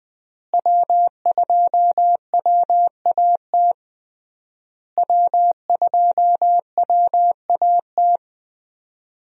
Morse code, heard as a keyed tone, about 19 wpm, 700 Hz. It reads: W2WAT W2WAT
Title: USA Call Signs - 5 Characters - Repeated 3x - 20wpm